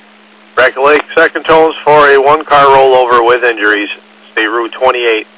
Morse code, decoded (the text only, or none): none